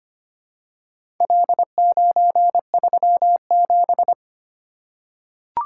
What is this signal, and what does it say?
Morse code, heard as a keyed tone, about 25 wpm, 700 Hz. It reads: L937
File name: practice / 25wpm / Four Letter-Number Combinations - 25wpm